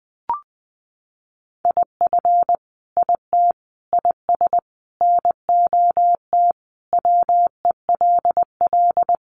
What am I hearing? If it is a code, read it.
Morse code, heard as a keyed tone, about 20 wpm, 700 Hz. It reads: IF IT IS NOT WELL
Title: Sentences from Top 1200 Words 20wpm